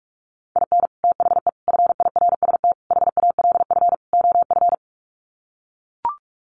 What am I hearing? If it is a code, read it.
Morse code, heard as a keyed tone, about 44 wpm, 700 Hz. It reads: IN THE FIRST HALF OF
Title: Sets of 5 Words 45wpm